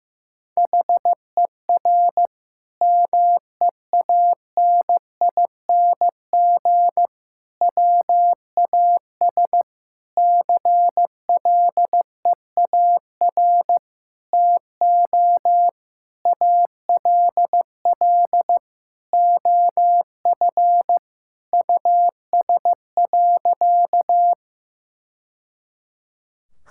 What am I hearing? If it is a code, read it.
Morse code, heard as a keyed tone, about 16 wpm, 700 Hz. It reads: HER MEANING WAS CLEAR TO ALL OF US.
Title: Sentences from Top 1000 Words - Review Part 6 - 15wpm